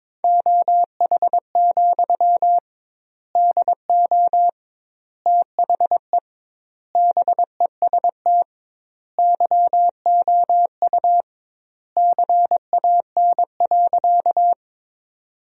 Morse code, heard as a keyed tone, about 22 wpm, 700 Hz. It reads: OH, DO THE BEST YOU CAN.
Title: 2x Word-Spacing - Sentences from Top 500 Words - 22wpm